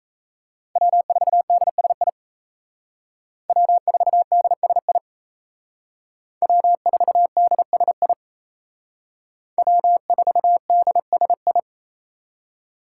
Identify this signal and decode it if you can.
Morse code, heard as a keyed tone, about 34 wpm, 700 Hz. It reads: W4DSI W4DSI W4DSI W4DSI
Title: Speed Racing - USA Call Signs 28wpm